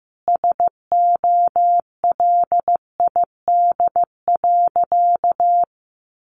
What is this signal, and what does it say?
Morse code, heard as a keyed tone, about 15 wpm, 700 Hz. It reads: SOLID.
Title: Sentences from Top 1500 Words 15wpm